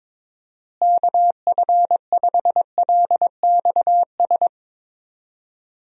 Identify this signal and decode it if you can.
Morse code, heard as a keyed tone, about 22 wpm, 700 Hz. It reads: KF5LXS
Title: USA Call Signs - 6 Characters - Repeated 2x - 22wpm